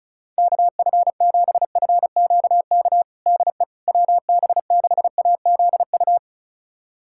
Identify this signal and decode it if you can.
Morse code, heard as a keyed tone, about 34 wpm, 700 Hz. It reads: KF7FQK DE WB6AZU
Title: USA Callsign DE Callsign - 6 Characters - 35wpm